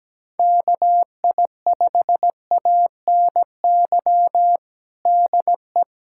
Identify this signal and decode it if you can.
Morse code, heard as a keyed tone, about 17 wpm, 700 Hz. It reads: KI5ANY DE
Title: USA Callsign DE Callsign 17wpm